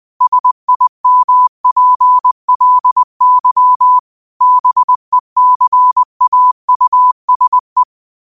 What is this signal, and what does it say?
Morse code, heard as a keyed tone, about 20 wpm, 1000 Hz. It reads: SIMPLY BECAUSE
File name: qotdfiles / 20wpm